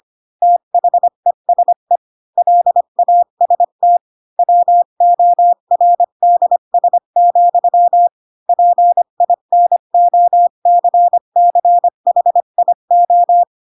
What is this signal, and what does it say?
Morse code, heard as a keyed tone, about 25 wpm, 700 Hz. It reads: THESE LAST WORDS, PINOCCHIO
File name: Pinocchio / Pinocchio ch4 20 25 ik1hge